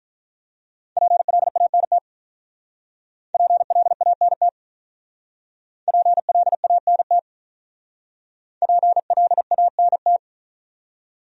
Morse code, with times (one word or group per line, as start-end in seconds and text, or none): 1.0-2.0 PLANT
3.3-4.5 PLANT
5.9-7.2 PLANT
8.6-10.2 PLANT